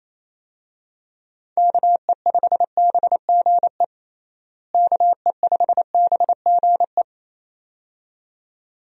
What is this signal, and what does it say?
Morse code, heard as a keyed tone, about 28 wpm, 700 Hz. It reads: KE5BGE KE5BGE
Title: USA Call Signs - 6 Characters - Repeated 2x - 28wpm